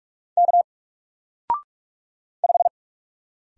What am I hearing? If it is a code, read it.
Morse code, heard as a keyed tone, about 44 wpm, 700 Hz. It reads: K 5
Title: Single Letter-Number 45wpm